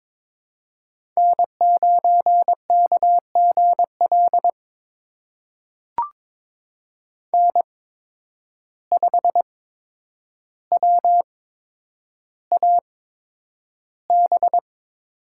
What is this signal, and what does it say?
Morse code, heard as a keyed tone, about 22 wpm, spaced as for 9 wpm, 700 Hz. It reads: N9KGL N5WAB
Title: USA Call Signs - 5 Characters - 8x Character Spacing - 22wpm